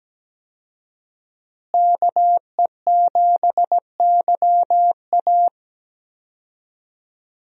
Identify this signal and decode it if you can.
Morse code, heard as a keyed tone, about 17 wpm, 700 Hz. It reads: KE7YA
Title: USA Call Signs - 5 Characters - Repeated 3x - 17wpm